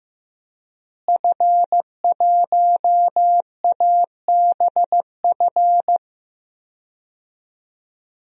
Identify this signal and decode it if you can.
Morse code, heard as a keyed tone, about 15 wpm, 700 Hz. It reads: F1ABF
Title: DXCC-100 Call Signs - 5 Characters - Repeated 3x - 15wpm